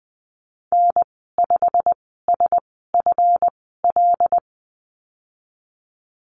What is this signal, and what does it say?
Morse code, heard as a keyed tone, about 19 wpm, spaced as for 14 wpm, 700 Hz. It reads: N5SFL
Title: 186 Lesson 30g - Letter b - Call Signs 20wpm